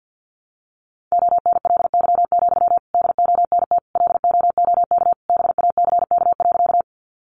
Morse code, heard as a keyed tone, about 49 wpm, 700 Hz. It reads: ONLY, DONT LOOK BACK.
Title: Sentences from Top 300 Words 50wpm